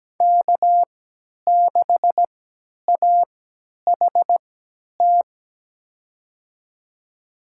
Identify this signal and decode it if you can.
Morse code, heard as a keyed tone, about 17 wpm, spaced as for 10 wpm, 700 Hz. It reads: K6AHT
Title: USA Call Signs - 5 Characters - 3x Character Spacing - 17wpm